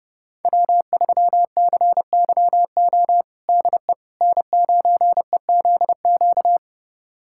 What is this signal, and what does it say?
Morse code, heard as a keyed tone, about 30 wpm, 700 Hz. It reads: W3CYO DE N9EZQ
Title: USA Callsign DE Callsign - 5 Characters - 30wpm